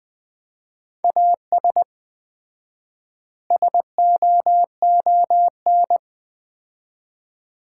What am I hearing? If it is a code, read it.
Morse code, heard as a keyed tone, about 20 wpm, 700 Hz. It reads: AS SOON
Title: 4x Word-Spacing - Sets of 3 Words - 20wpm